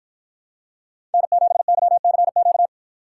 Morse code, heard as a keyed tone, about 53 wpm, 700 Hz. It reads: N7YCQ